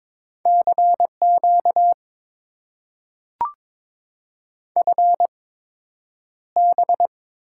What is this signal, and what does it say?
Morse code, heard as a keyed tone, about 22 wpm, spaced as for 9 wpm, 700 Hz. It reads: CQ FB
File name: Top 90 QSO Elements - 8x Character Spacing - 22wpm